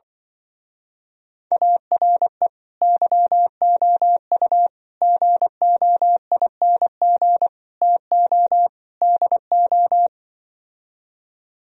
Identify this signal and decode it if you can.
Morse code, heard as a keyed tone, about 24 wpm, 700 Hz. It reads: ARE YOU GOING TO DO
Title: Speed-Racing - Sets of 5 Words 20wpm